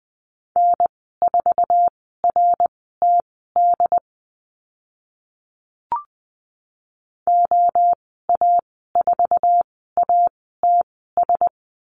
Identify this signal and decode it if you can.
Morse code, heard as a keyed tone, about 20 wpm, spaced as for 14 wpm, 700 Hz. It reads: N4RTD OA4ATS
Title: 069 Lesson 12g - Letter d - Call Signs 20wpm